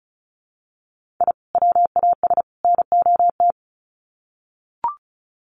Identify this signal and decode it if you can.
Morse code, heard as a keyed tone, about 34 wpm, 700 Hz. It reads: I WAS NOT
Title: Sets of 3 Words 35wpm